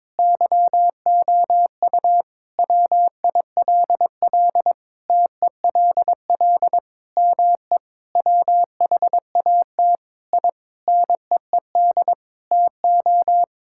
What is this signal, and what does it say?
Morse code, heard as a keyed tone, about 22 wpm, 700 Hz. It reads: YOU WILL TELL ME WHAT I NEED TO
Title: Sentences from Top 1000 Words - Review Part 5 - 22wpm